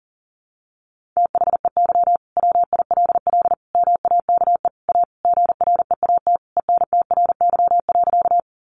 Morse code, heard as a keyed tone, about 40 wpm, 700 Hz. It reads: THEY WILL MAKE A GREAT ENTRY.
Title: Sentences from Top 1500 Words 40wpm